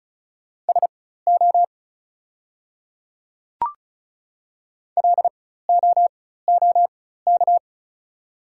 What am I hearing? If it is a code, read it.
Morse code, heard as a keyed tone, about 36 wpm, spaced as for 16 wpm, 700 Hz. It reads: SO LOOK
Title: 4x Character Spacing - Top 200 Words - 35wpm